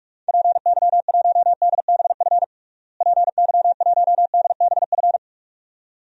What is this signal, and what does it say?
Morse code, heard as a keyed tone, about 46 wpm, 700 Hz. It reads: PY1DBF PY1DBF
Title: DXCC-100 Call Signs - 6 Characters - Repeated 2x - 45wpm